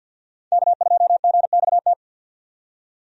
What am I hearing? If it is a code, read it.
Morse code, heard as a keyed tone, about 50 wpm, 700 Hz. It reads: K2GXT
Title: USA Call Signs - 5 Characters - 50wpm